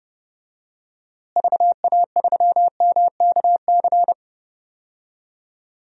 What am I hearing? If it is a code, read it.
Morse code, heard as a keyed tone, about 31 wpm, 700 Hz. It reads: VA3MKC
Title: DXCC-100 Call Signs - 6 Characters - Repeated 3x - 30wpm